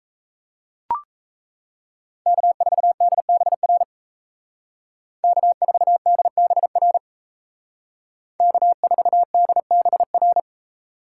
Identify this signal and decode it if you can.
Morse code, heard as a keyed tone, about 33 wpm, 700 Hz. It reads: K4DBR K4DBR K4DBR
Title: Speed Racing - USA 1x3 Call Signs 28wpm